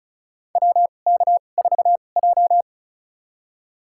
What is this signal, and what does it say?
Morse code, heard as a keyed tone, about 34 wpm, spaced as for 25 wpm, 700 Hz. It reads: WK4J
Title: USA Call Signs - 4 Characters - 2x Character Spacing - 35wpm